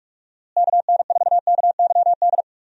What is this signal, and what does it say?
Morse code, heard as a keyed tone, about 46 wpm, 700 Hz. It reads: KN4KYD
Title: USA Call Signs - 6 Characters - Repeated 2x - 45wpm